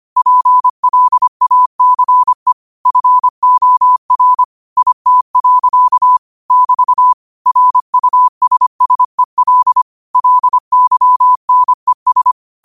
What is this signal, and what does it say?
Morse code, heard as a keyed tone, about 25 wpm, 1000 Hz. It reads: PLACE FOR IT. <BT> RUSSEL LYNES